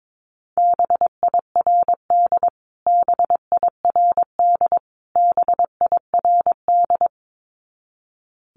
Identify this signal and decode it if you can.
Morse code, heard as a keyed tone, about 22 wpm, 700 Hz. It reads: BIRD BIRD BIRD
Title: Top 500 Words Repeated 22wpm